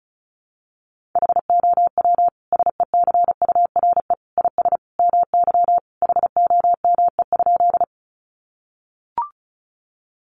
Morse code, heard as a keyed tone, about 34 wpm, 700 Hz. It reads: HOW SECURE IS MY HOME?
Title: Sentences from Top 1200 Words 35wpm